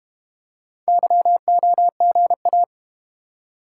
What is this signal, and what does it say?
Morse code, heard as a keyed tone, about 34 wpm, 700 Hz. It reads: YOGA